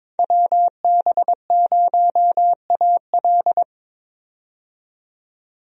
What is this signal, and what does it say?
Morse code, heard as a keyed tone, about 22 wpm, 700 Hz. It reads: WB0AL
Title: USA Call Signs - 5 Characters - 22wpm